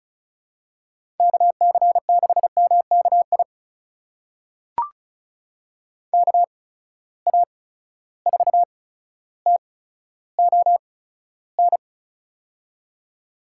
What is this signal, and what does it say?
Morse code, heard as a keyed tone, about 36 wpm, spaced as for 15 wpm, 700 Hz. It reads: KC6MKI KA4TON
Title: USA Call Signs - 6 Characters - 8x Character Spacing - 35wpm